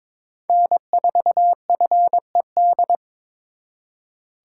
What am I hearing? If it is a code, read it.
Morse code, heard as a keyed tone, about 26 wpm, 700 Hz. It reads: N4FED
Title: Speed Racing - USA Call Signs 22wpm